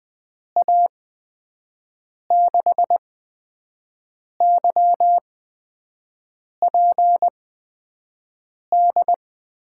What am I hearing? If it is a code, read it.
Morse code, heard as a keyed tone, about 20 wpm, spaced as for 5 wpm, 700 Hz. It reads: A6YPD
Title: USA Call Signs - 6 Characters - 8x Character Spacing - 20wpm